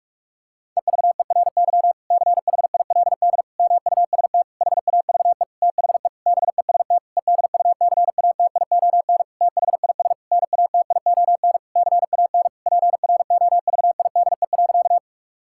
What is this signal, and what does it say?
Morse code, heard as a keyed tone, about 46 wpm, 700 Hz. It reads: EVERY CHILD MUST HAVE THE BEST EDUCATION THIS NATION CAN PROVIDE.
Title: Sentences from Top 1000 Words - Review Part 6 - 45wpm